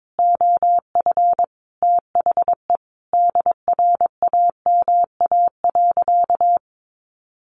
Morse code, heard as a keyed tone, about 22 wpm, 700 Hz. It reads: OF THE DRAMA.
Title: Sentences from Top 1500 Words 22wpm